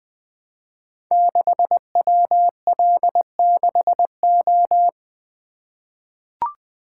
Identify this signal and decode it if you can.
Morse code, heard as a keyed tone, about 20 wpm, 700 Hz. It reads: 6WL6O
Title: Code Groups - 5 Letter:Numbers - 20wpm